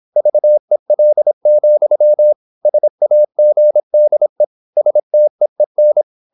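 Morse code, heard as a keyed tone, about 26 wpm, 600 Hz. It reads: VEL, SAGDE STEEN
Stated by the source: High-speed (QRQ) Morse at 26 wpm.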